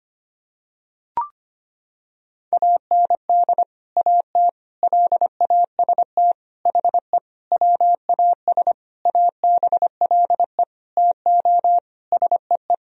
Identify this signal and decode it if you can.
Morse code, heard as a keyed tone, about 25 wpm, 700 Hz. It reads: AND AT LAST HE WAS ABLE TO SEE
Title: Sentences from Top 1000 Words - Review Part 2 - 25wpm